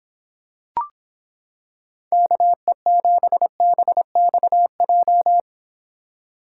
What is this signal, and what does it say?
Morse code, heard as a keyed tone, about 26 wpm, 700 Hz. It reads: KE7BXJ